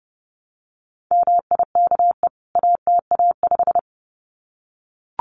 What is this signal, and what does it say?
Morse code, heard as a keyed tone, about 30 wpm, 700 Hz. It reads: MIKE ATA5
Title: CWT Exchange 30wpm